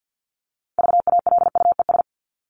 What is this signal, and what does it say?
Morse code, heard as a keyed tone, about 50 wpm, 700 Hz. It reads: VALUES
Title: ES:S Suffix - Top 100 Words with Base Words 50wpm